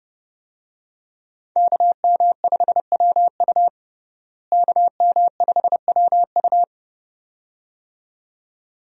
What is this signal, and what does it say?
Morse code, heard as a keyed tone, about 30 wpm, 700 Hz. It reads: KM5WU KM5WU
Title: USA Call Signs - 5 Characters - Repeated 2x - 30wpm